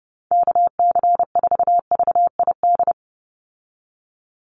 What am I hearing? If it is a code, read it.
Morse code, heard as a keyed tone, about 30 wpm, 700 Hz. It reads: KC4VID
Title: USA Call Signs 30wpm